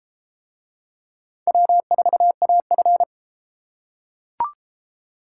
Morse code, heard as a keyed tone, about 32 wpm, 700 Hz. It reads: W4AF